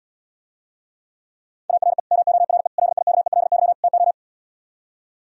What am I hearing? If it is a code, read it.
Morse code, heard as a keyed tone, about 77 wpm, 700 Hz. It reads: ARE MORE LIKELY TO